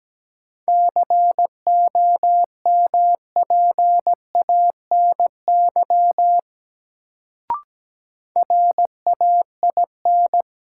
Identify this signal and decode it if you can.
Morse code, heard as a keyed tone, about 17 wpm, 700 Hz. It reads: COMPANY RAIN
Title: Sentences from Top 600 Words - New Word Drill - 17wpm